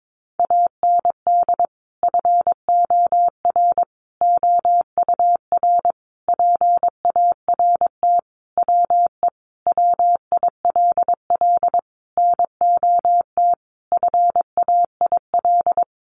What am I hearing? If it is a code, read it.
Morse code, heard as a keyed tone, about 22 wpm, 700 Hz. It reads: AND FOR OUR PART WE WILL NOT FAIL
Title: Sentences from Top 1100 Words 22wpm